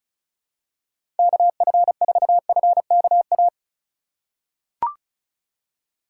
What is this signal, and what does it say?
Morse code, heard as a keyed tone, about 34 wpm, 700 Hz. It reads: KF4FKA
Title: USA Call Signs - 6 Characters - 2x Character Spacing - 35wpm